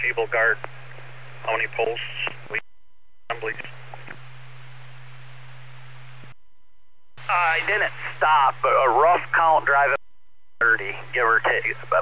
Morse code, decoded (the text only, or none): none